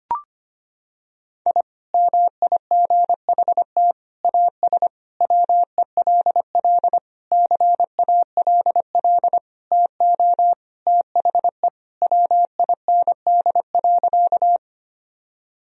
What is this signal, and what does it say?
Morse code, heard as a keyed tone, about 25 wpm, 700 Hz. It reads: I MIGHT AS WELL CALL TO THE WIND.
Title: Sentences from Top 1000 Words - Review Part 2 - 25wpm